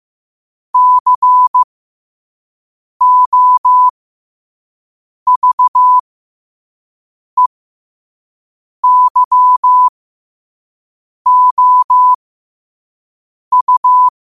Morse code, heard as a keyed tone, about 15 wpm, spaced as for 5 wpm, 1000 Hz. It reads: COVEYOU
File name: qotdfiles / 5wpm